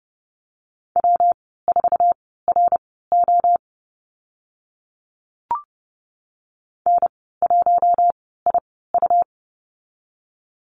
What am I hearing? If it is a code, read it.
Morse code, heard as a keyed tone, about 30 wpm, spaced as for 17 wpm, 700 Hz. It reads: W4RO N1IU
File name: Lesson 19g - Letter w - Call Signs 30wpm